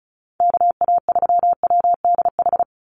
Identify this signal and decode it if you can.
Morse code, heard as a keyed tone, about 34 wpm, 700 Hz. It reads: KA3WDH
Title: USA Call Signs 35wpm